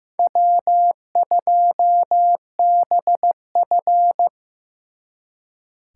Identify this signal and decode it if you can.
Morse code, heard as a keyed tone, about 18 wpm, 700 Hz. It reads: W2BF
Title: Speed Racing - USA 1x2 Call Signs 15wpm